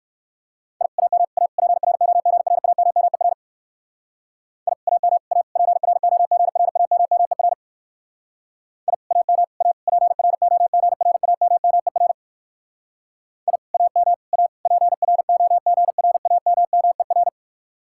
Morse code, heard as a keyed tone, about 63 wpm, 700 Hz. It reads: I AM A PROGRAMMER I AM A PROGRAMMER I AM A PROGRAMMER I AM A PROGRAMMER